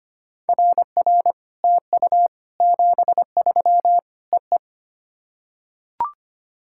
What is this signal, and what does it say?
Morse code, heard as a keyed tone, about 25 wpm, 700 Hz. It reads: RR TU 73 EE
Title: SOTA QSO Practice 25wpm